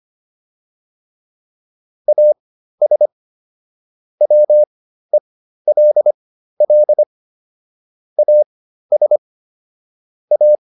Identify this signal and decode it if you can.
Morse code, heard as a keyed tone, about 25 wpm, spaced as for 13 wpm, 600 Hz. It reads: AS WELL AS A